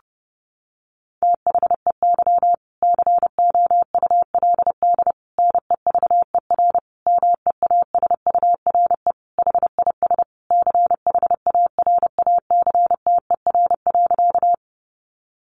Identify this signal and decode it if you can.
Morse code, heard as a keyed tone, about 30 wpm, 700 Hz. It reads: THEY COULD NEVER MEASURE HIS CHARACTER.
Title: Sentences from Top 1200 Words 30wpm